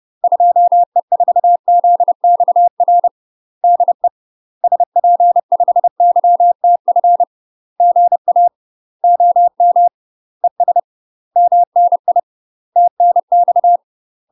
Morse code, heard as a keyed tone, about 30 wpm, 700 Hz. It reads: 2E4ZXR DE SP5YTF GA OM ES MNI TNX
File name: UK MASTER QSOs TRAINING / 30WPM / UK MASTER QSOs 0031